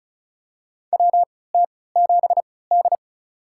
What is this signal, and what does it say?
Morse code, heard as a keyed tone, about 36 wpm, spaced as for 20 wpm, 700 Hz. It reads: WT7D
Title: USA Call Signs - 4 Characters - 3x Character Spacing - 35wpm